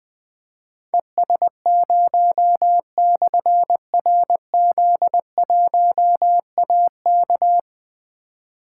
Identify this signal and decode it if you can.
Morse code, heard as a keyed tone, about 20 wpm, 700 Hz. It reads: ES0/RZ1AK
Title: Difficult Call Signs 20wpm